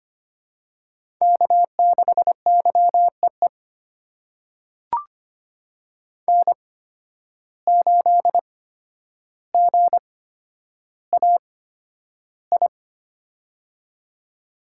Morse code, heard as a keyed tone, about 25 wpm, spaced as for 11 wpm, 700 Hz. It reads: K6YEE N8GAI
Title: USA Call Signs - 5 Characters - 8x Character Spacing - 25wpm